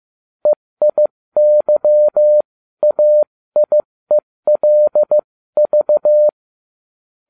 Morse code, heard as a keyed tone, about 15 wpm, spaced as for 13 wpm, 600 Hz. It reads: EIYAIELV